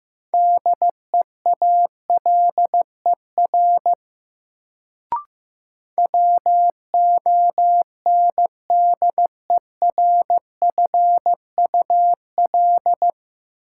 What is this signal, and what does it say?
Morse code, heard as a keyed tone, about 15 wpm, 700 Hz. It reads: DEALER WONDERFUL
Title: Sentences from Top 1100 Words - New Word Drill - 15wpm